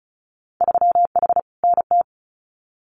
Morse code, heard as a keyed tone, about 36 wpm, 700 Hz. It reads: 3H NT
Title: Winter Field Day Exchanges Encore 35wpm